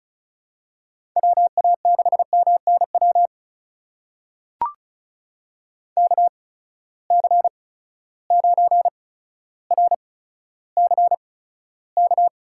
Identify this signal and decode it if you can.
Morse code, heard as a keyed tone, about 36 wpm, spaced as for 15 wpm, 700 Hz. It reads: WA6MNW KC9RCK
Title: USA Call Signs - 6 Characters - 8x Character Spacing - 35wpm